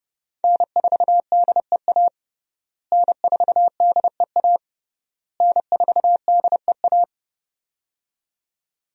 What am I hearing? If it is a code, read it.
Morse code, heard as a keyed tone, about 32 wpm, 700 Hz. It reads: N4DEA N4DEA N4DEA
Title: USA Call Signs - 5 Characters - Repeated 3x - 30wpm